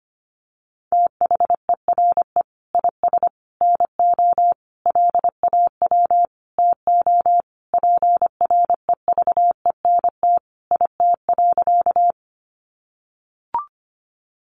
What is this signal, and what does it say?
Morse code, heard as a keyed tone, about 25 wpm, 700 Hz. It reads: THERE IS NO LAW TO PREVENT IT.
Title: Sentences from Top 1100 Words 25wpm